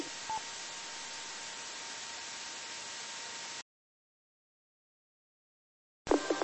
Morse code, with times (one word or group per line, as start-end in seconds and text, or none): none